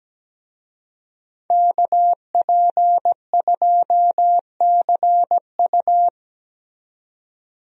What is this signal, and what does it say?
Morse code, heard as a keyed tone, about 17 wpm, 700 Hz. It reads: KP2CU
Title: DXCC-100 Call Signs - 5 Characters - 17wpm